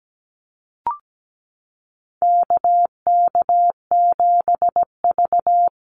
Morse code, heard as a keyed tone, about 17 wpm, 700 Hz. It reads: KK7V
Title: USA Call Signs Repeated 2x 17wpm